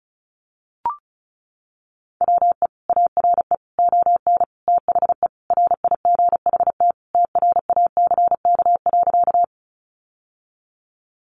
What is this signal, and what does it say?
Morse code, heard as a keyed tone, about 34 wpm, 700 Hz. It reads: WE ARE ON THE RIGHT TRACK.
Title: Sentences from Top 1100 Words 35wpm